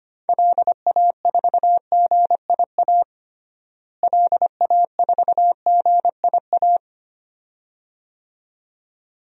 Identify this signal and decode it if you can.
Morse code, heard as a keyed tone, about 25 wpm, 700 Hz. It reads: LA4GIA LA4GIA
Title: DXCC-100 Call Signs - 6 Characters - Repeated 2x - 25wpm